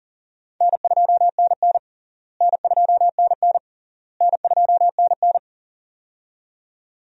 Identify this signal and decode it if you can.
Morse code, heard as a keyed tone, about 40 wpm, 700 Hz. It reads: N2NN N2NN N2NN
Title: USA Call Signs - 4 Characters - Repeated 3x - 40wpm